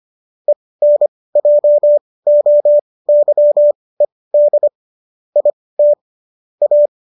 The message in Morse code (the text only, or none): ENJOYED IT A